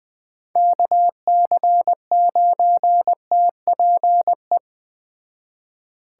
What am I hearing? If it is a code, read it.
Morse code, heard as a keyed tone, about 20 wpm, 700 Hz. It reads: KC9TPE
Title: Speed Racing - USA Call Signs 17wpm